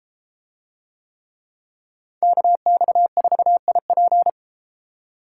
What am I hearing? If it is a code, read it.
Morse code, heard as a keyed tone, about 30 wpm, 700 Hz. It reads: KX4IP